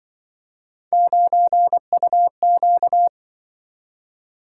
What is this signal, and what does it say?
Morse code, heard as a keyed tone, about 24 wpm, 700 Hz. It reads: 9UQ
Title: Speed-Racing - Three Letter-Number Combinations - 20wpm